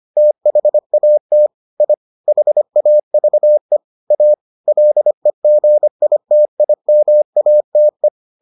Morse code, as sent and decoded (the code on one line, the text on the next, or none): - .... .- - / .. / .... .- ...- . / .- / .-.. --. .. - .. -- .- - .
THAT I HAVE A LEGITIMATE